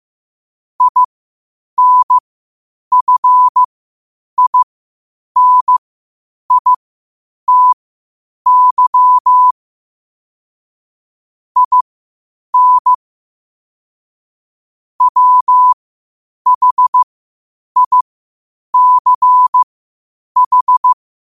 Morse code, heard as a keyed tone, about 15 wpm, spaced as for 8 wpm, 1000 Hz. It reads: INFINITY IN WHICH
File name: qotdfiles / 8wpm